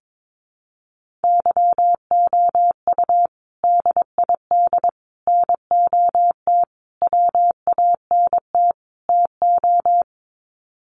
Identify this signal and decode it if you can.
Morse code, heard as a keyed tone, about 22 wpm, 700 Hz. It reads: YOU DID NOT WANT TO
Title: Sets of 5 Words Encore2 22wpm